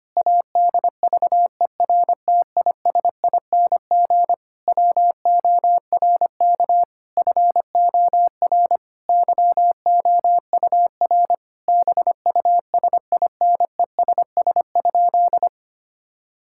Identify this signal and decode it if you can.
Morse code, heard as a keyed tone, about 25 wpm, 700 Hz. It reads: ADVERTISING WORK FOR YOUR BUSINESS?
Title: Sentences from Top 1000 Words - Review Part 6 - 25wpm